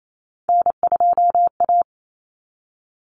Morse code, heard as a keyed tone, about 28 wpm, 700 Hz. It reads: N2A
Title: Common Cut Numbers - 3-Digit 28wpm